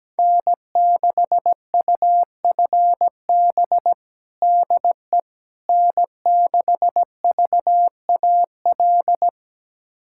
Morse code, heard as a keyed tone, about 17 wpm, 700 Hz. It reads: N6UFB DE N6VAL
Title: USA Callsign DE Callsign - 5 Characters - 17wpm